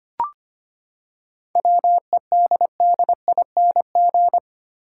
Morse code, heard as a keyed tone, about 25 wpm, 700 Hz. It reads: WEDDING
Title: Seven Letter Words - 25wpm